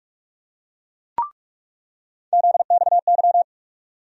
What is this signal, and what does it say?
Morse code, heard as a keyed tone, about 46 wpm, 700 Hz. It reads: ZXY
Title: USA Call Sign Postfixes - 3 Letters 45wpm